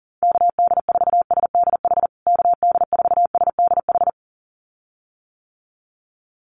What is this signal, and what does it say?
Morse code, heard as a keyed tone, about 40 wpm, 700 Hz. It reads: KD4SDH KD4SDH
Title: USA Call Signs Repeated 2x 40wpm